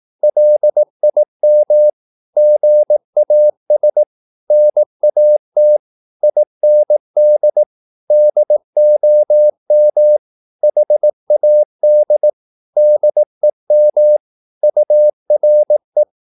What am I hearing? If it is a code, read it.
Morse code, heard as a keyed tone, about 18 wpm, 600 Hz. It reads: LIM GAS NAT IND DOM HAD DEM URE
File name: Korte ord DK 18wpm